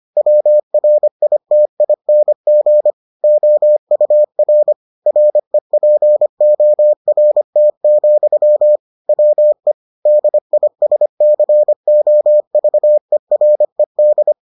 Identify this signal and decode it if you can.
Morse code, heard as a keyed tone, about 25 wpm, 600 Hz. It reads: WRITING OUR REPORT, WE DISCOVERED